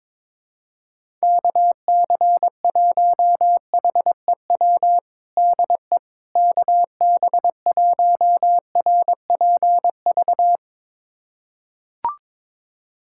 Morse code, heard as a keyed tone, about 22 wpm, 700 Hz. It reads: KC1HEW DE KB1RPV
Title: USA Callsign DE Callsign - 6 Characters - 22wpm